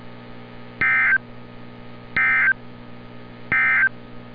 Emergency Alert System (SAME) tones, heard as harsh buzzing bursts. It is heard from 0.7 to 4.0 s.
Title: NOAA Weather Radio - Altoona, PA